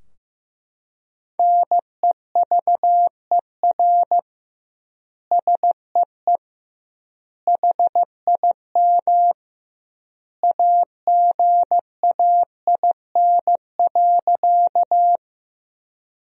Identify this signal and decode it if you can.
Morse code, heard as a keyed tone, about 15 wpm, 700 Hz. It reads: NEVER SEE HIM AGAIN.
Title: 2x Word-Spacing - Sentences from Top 300 Words - 15wpm